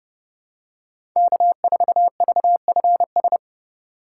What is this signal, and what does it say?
Morse code, heard as a keyed tone, about 30 wpm, 700 Hz. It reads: K4VFS
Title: USA Call Signs - 5 Characters - 3x Character Spacing - 30wpm